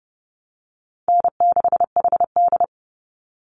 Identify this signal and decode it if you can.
Morse code, heard as a keyed tone, about 30 wpm, 700 Hz. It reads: N6HD
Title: Easy Call Signs Repeated 3x - 30wpm